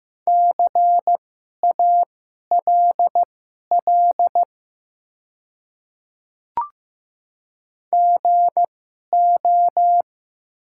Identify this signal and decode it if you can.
Morse code, heard as a keyed tone, about 15 wpm, spaced as for 11 wpm, 700 Hz. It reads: CALL GO
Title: 2x Character Spacing - Top 100 Words - 15wpm